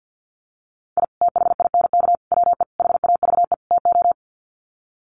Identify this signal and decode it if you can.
Morse code, heard as a keyed tone, about 50 wpm, 700 Hz. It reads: I THINK WE HAVE TO
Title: Sets of 5 Words 50wpm